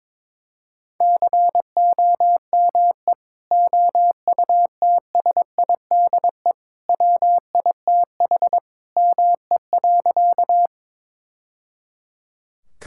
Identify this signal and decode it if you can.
Morse code, heard as a keyed tone, about 21 wpm, 700 Hz. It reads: COME OUTSIDE WITH ME.
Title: Sentences from Top 1000 Words - Review Part 6 - 22wpm